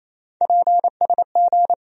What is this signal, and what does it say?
Morse code, heard as a keyed tone, about 28 wpm, 700 Hz. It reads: PSG
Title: USA Call Sign Postfixes - 3 Letters 28wpm